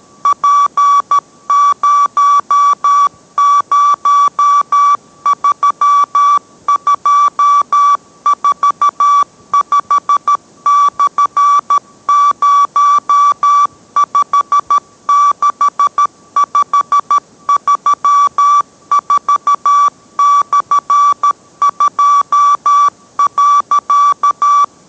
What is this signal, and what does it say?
Morse code, heard as a keyed tone, about 13 wpm, 1150 Hz. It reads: P003245/056534/2.